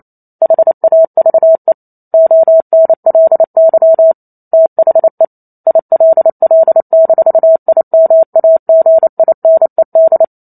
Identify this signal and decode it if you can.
Morse code, heard as a keyed tone, about 28 wpm, 650 Hz. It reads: HAVE ONLY THE ILL-IMAGINED